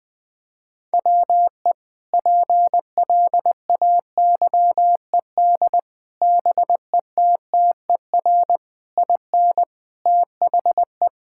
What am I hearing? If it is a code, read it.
Morse code, heard as a keyed tone, about 21 wpm, 700 Hz. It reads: WE PLAYED BETTER IN THE
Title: Sentences with 5-Word Phrases - 20wpm